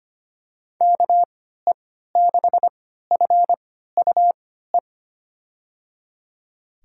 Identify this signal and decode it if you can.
Morse code, heard as a keyed tone, about 25 wpm, spaced as for 14 wpm, 700 Hz. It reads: KE6FUE
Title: USA Call Signs - 3x Character Spacing - 25wpm